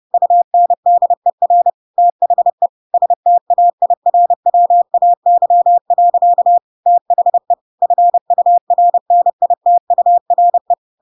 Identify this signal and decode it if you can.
Morse code, heard as a keyed tone, about 30 wpm, 700 Hz. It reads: UNDER THE STAIRWAY. THE FURNITURE
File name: Pinocchio / Pinocchio ch3 30 30 ik1hge